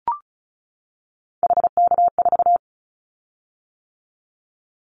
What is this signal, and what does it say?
Morse code, heard as a keyed tone, about 34 wpm, 700 Hz. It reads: HK4
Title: Callsign Prefixes for Top 100 DXCC - 35wpm